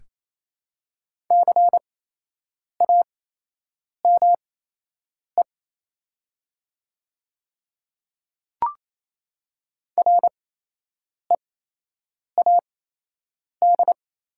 Morse code, heard as a keyed tone, about 28 wpm, spaced as for 8 wpm, 700 Hz. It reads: CAME READ
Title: 8x Character Spacing - Top 200 Words - 28wpm